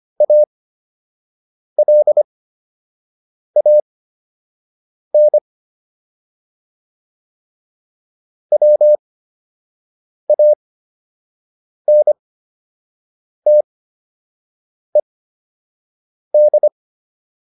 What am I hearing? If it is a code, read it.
Morse code, heard as a keyed tone, about 25 wpm, spaced as for 6 wpm, 600 Hz. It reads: ALAN WANTED